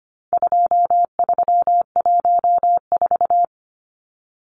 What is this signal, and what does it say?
Morse code, heard as a keyed tone, about 25 wpm, 700 Hz. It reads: 2314
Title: Numbers - 4-Digit 25wpm